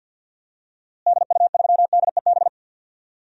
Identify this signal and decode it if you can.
Morse code, heard as a keyed tone, about 49 wpm, 700 Hz. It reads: DU3DEB
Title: DXCC-100 Call Signs - 6 Characters - Repeated 3x - 50wpm